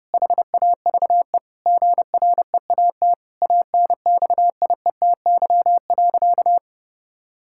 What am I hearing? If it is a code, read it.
Morse code, heard as a keyed tone, about 32 wpm, 700 Hz. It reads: HAVE GREAT ANXIETY.
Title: Sentences from Top 1600 Words 30wpm